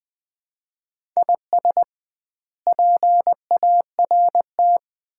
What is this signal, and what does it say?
Morse code, heard as a keyed tone, about 20 wpm, 700 Hz. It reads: IS PART
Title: 2x Word-Spacing - Sets of 4 Words - 20wpm